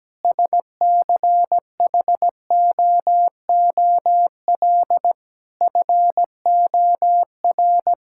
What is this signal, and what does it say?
Morse code, heard as a keyed tone, about 17 wpm, 700 Hz. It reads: SCHOOL FOR
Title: Sentences from Top 1000 Words - Review Part 6 - 17wpm